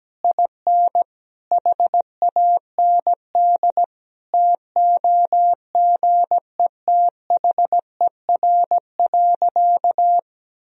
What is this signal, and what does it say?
Morse code, heard as a keyed tone, about 17 wpm, 700 Hz. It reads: IN HAND TOGETHER.
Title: Sentences from Top 1000 Words - Review Part 3 - 17wpm